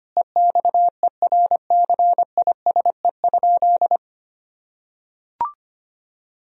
Morse code, heard as a keyed tone, about 25 wpm, 700 Hz. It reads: EXERCISE?
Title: Sentences from Top 1000 Words - Review Part 4 - 25wpm